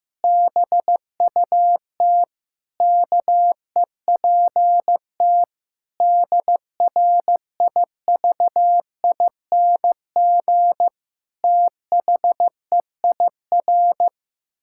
Morse code, heard as a keyed tone, about 15 wpm, 700 Hz. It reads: BUT KEPT DRIVING THEIR
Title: Pangram Challenge v2 - 15wpm